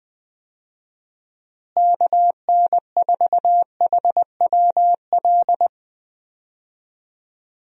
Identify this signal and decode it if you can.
Morse code, heard as a keyed tone, about 21 wpm, 700 Hz. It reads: KN4HWL
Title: USA Call Signs - 6 Characters - Repeated 3x - 20wpm